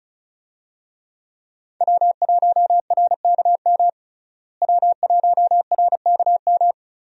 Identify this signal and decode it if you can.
Morse code, heard as a keyed tone, about 34 wpm, 700 Hz. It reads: W1RKM W1RKM
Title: USA Call Signs - 5 Characters - Repeated 2x - 35wpm